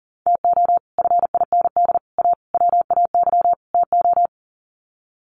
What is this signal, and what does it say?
Morse code, heard as a keyed tone, about 40 wpm, 700 Hz. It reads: TO FIND A WAY TO
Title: Sets of 5 Words 40wpm